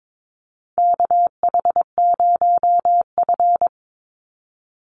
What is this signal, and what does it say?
Morse code, heard as a keyed tone, about 22 wpm, 700 Hz. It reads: KH0F
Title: USA Call Signs 22wpm